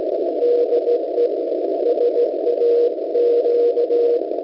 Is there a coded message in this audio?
Yes: Morse code from 0.4 to 4.2 s, heard as a keyed tone, about 17 wpm, 480 Hz, with noise. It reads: DE SAQ